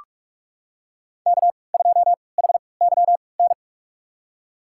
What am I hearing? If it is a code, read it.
Morse code, heard as a keyed tone, about 43 wpm, spaced as for 25 wpm, 700 Hz. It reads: K2HYN
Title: USA Call Signs - 5 Characters - 3x Character Spacing - 45wpm